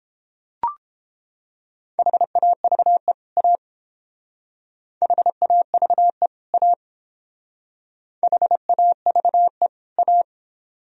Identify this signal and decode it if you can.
Morse code, heard as a keyed tone, about 29 wpm, 700 Hz. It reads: HAVE A HAVE A HAVE A